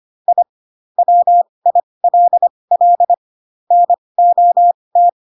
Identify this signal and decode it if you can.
Morse code, heard as a keyed tone, about 25 wpm, spaced as for 20 wpm, 700 Hz. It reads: I WILL NOT
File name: Pinocchio / Pinocchio ch7 15 25 ik1hge